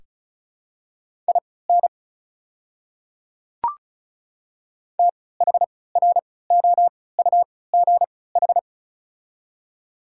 Morse code, heard as a keyed tone, about 36 wpm, spaced as for 20 wpm, 700 Hz. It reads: IN THROUGH
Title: 3x Character Spacing - Top 500 Words - 35wpm